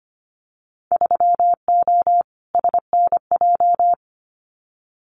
Morse code, heard as a keyed tone, about 25 wpm, 700 Hz. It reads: 3O SNJ
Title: Winter Field Day Exchanges Encore 25wpm